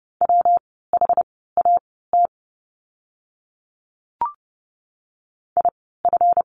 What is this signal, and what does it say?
Morse code, heard as a keyed tone, about 30 wpm, spaced as for 17 wpm, 700 Hz. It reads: WHAT IF
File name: course / 30:10wpm / 230 Lesson 37f - Letter x - Words 30wpm